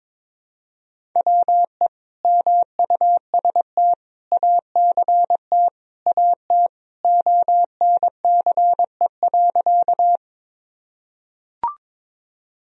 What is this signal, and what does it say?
Morse code, heard as a keyed tone, about 22 wpm, 700 Hz. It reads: WE MUST ACT AT ONCE.
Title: Sentences from Top 1000 Words - Review Part 1 - 22wpm